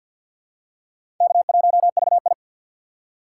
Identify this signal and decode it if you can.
Morse code, heard as a keyed tone, about 53 wpm, 700 Hz. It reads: K1VI
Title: USA Call Signs - 4 Characters - 6x Character Spacing - 50wpm